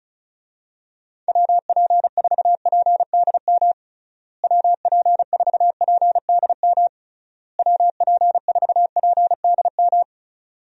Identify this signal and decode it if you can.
Morse code, heard as a keyed tone, about 36 wpm, 700 Hz. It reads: WP4PDM WP4PDM WP4PDM
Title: USA Call Signs - 6 Characters - Repeated 3x - 35wpm